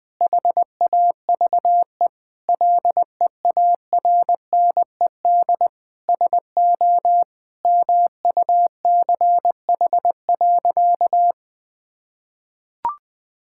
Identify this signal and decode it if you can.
Morse code, heard as a keyed tone, about 20 wpm, 700 Hz. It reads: HAVE LEARNED SO MUCH.
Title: Sentences with 5-Word Phrases - 20wpm